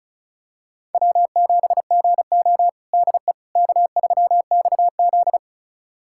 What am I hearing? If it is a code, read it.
Morse code, heard as a keyed tone, about 36 wpm, 700 Hz. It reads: W7GO DE K3XZ
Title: USA Callsign DE Callsign - 4 Characters - 35wpm